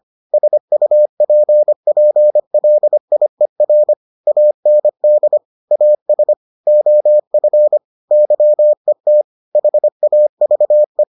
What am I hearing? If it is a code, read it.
Morse code, heard as a keyed tone, about 25 wpm, 600 Hz. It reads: SUPPLIER AND AS OF YET HAVE